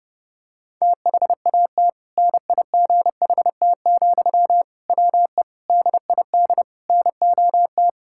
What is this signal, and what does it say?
Morse code, heard as a keyed tone, about 30 wpm, 700 Hz. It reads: THAT NIGHT, WE DID NOT
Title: Sentences from Top 1000 Words - Review Part 6 - 30wpm